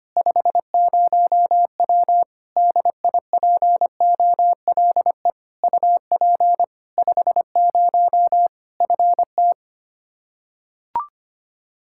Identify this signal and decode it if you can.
Morse code, heard as a keyed tone, about 25 wpm, 700 Hz. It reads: 50W DIPOLE UP 50 FT
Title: Practice QSO - Exchange 2 - Info - 25wpm